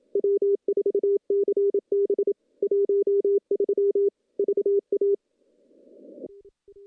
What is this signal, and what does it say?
Morse code, heard as a keyed tone, about 27 wpm, 410 Hz. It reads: W4CB 13 VA
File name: CW-SS-k5tr